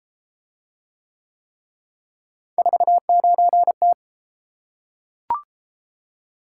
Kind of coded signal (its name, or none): Morse code